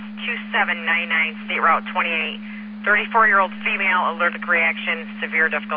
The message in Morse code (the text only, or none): none